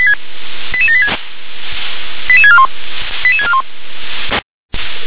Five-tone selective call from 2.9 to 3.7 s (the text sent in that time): 88641